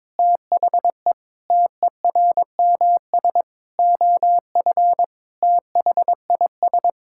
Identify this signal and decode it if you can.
Morse code, heard as a keyed tone, about 22 wpm, 700 Hz. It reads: THE TERMS OF THIS
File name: Sentences with 9-Word Phrases - 22wpm